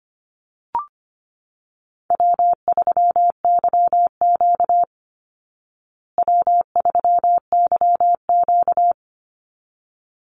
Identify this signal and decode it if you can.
Morse code, heard as a keyed tone, about 25 wpm, 700 Hz. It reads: W3YQ W3YQ
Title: Easy Call Signs Repeated 2x - 25wpm